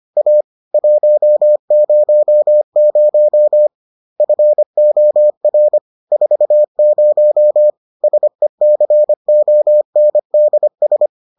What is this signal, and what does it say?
Morse code, heard as a keyed tone, about 25 wpm, 600 Hz. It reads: A 100 FOR 40 SECONDS